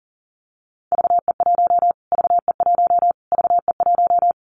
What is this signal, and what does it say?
Morse code, heard as a keyed tone, about 38 wpm, 700 Hz. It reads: VE1 VE1 VE1
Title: Callsign Prefixes for Top 25 DXCC - Repeated 3x - 40wpm